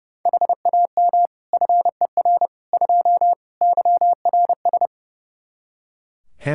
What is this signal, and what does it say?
Morse code, heard as a keyed tone, about 30 wpm, 700 Hz. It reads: HAM FER 2 YRS
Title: Practice QSO - Exchange 3 - More Info - 30wpm